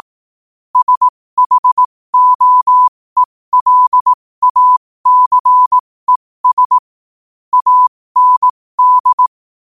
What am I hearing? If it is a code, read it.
Morse code, heard as a keyed tone, about 18 wpm, spaced as for 15 wpm, 1000 Hz. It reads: SHOELACES AND
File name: qotdfiles / 15wpm